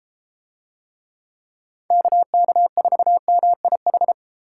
Morse code, heard as a keyed tone, about 31 wpm, 700 Hz. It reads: KK4MIH